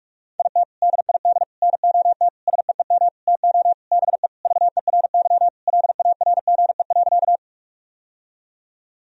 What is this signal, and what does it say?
Morse code, heard as a keyed tone, about 41 wpm, 700 Hz. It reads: IT DID NOT SEEM TO BE VERY LARGE.
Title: Sentences from Top 1000 Words - Review Part 1 - 45wpm